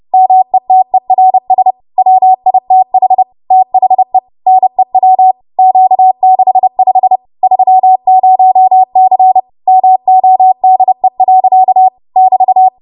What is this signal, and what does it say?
Morse code, heard as a keyed tone, about 30 wpm, 750 Hz. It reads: METERS WITH THE NEW Q65 30C MODE. <BT>
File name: Morse / 30WPM